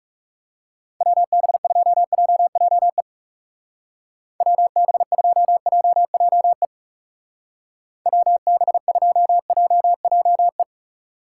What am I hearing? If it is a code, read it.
Morse code, heard as a keyed tone, about 36 wpm, 700 Hz. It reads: WB2JJE WB2JJE WB2JJE
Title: Speed Racing - USA 2x3 Call Signs 30wpm